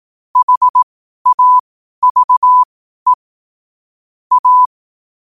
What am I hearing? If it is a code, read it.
Morse code, heard as a keyed tone, about 18 wpm, spaced as for 12 wpm, 1000 Hz. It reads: HAVE A
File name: qotdfiles / 12wpm